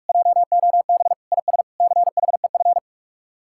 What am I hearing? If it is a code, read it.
Morse code, heard as a keyed tone, about 44 wpm, 700 Hz. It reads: JOB IS CHEF